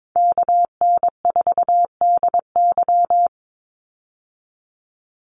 Morse code, heard as a keyed tone, about 22 wpm, 700 Hz. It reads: KN4DY
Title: USA Call Signs 22wpm